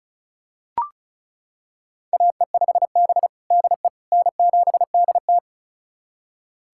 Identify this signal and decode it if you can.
Morse code, heard as a keyed tone, about 36 wpm, 700 Hz. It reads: AE5B DE N7DT
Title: USA Callsign DE Callsign - 4 Characters - 35wpm